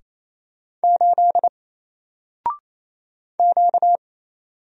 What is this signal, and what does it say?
Morse code, heard as a keyed tone, about 28 wpm, 700 Hz. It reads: 8 Q
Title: Single Letter-Number Alt 28wpm